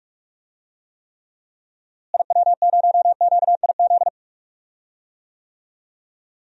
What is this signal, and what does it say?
Morse code, heard as a keyed tone, about 46 wpm, 700 Hz. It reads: IW0QIZ
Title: DXCC-100 Call Signs - 6 Characters - Repeated 3x - 45wpm